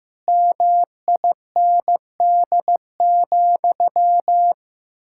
Morse code, heard as a keyed tone, about 15 wpm, 700 Hz. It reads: MIND,
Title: Sentences from Top 1000 Words - Review Part 6 - 15wpm